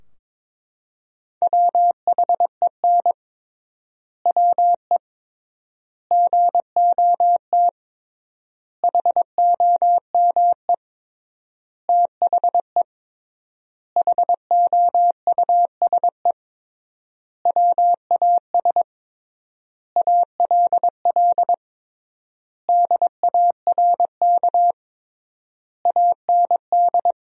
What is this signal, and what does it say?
Morse code, heard as a keyed tone, about 22 wpm, 700 Hz. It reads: WHEN WE GOT HOME THE HOUSE WAS ALL DARK AND
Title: 3x Word-Spacing - Sentences from Top 500 Words - 22wpm